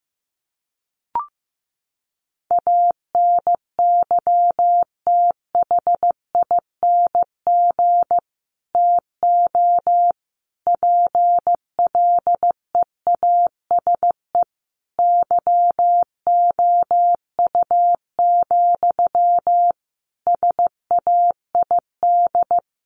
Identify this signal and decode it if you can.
Morse code, heard as a keyed tone, about 14 wpm, 700 Hz. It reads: ANYTHING TO PLEASE YOU, SAID
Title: Sentences from Top 1500 Words 15wpm